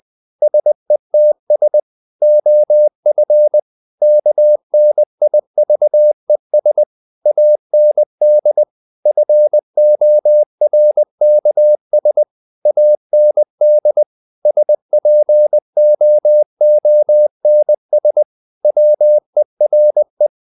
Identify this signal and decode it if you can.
Morse code, heard as a keyed tone, about 20 wpm, 600 Hz. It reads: SETS OF KNIVES AND FORKS AND SPOONS WERE
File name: the old man and the sea / the old man and the sea part 1 20wpm